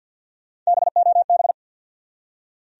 Morse code, heard as a keyed tone, about 53 wpm, 700 Hz. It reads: BOB
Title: USA Common Names - Max 3 Letters - 50wpm